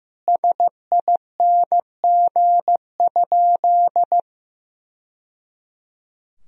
Morse code, heard as a keyed tone, about 15 wpm, 700 Hz. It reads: SING?
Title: Sentences from Top 1000 Words - Review Part 2 - 15wpm